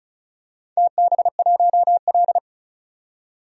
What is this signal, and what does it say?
Morse code, heard as a keyed tone, about 36 wpm, 700 Hz. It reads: TB1L
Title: DXCC-100 Call Signs - 4 Characters - 35wpm